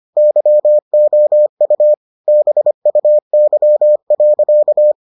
Morse code, heard as a keyed tone, about 25 wpm, 600 Hz. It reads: YOU BUY.